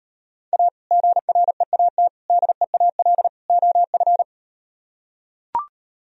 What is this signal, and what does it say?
Morse code, heard as a keyed tone, about 38 wpm, 700 Hz. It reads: A GREAT DEAL OF